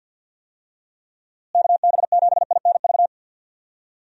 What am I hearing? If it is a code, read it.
Morse code, heard as a keyed tone, about 50 wpm, 700 Hz. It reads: KB7INV
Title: USA Call Signs - 6 Characters - Repeated 2x - 50wpm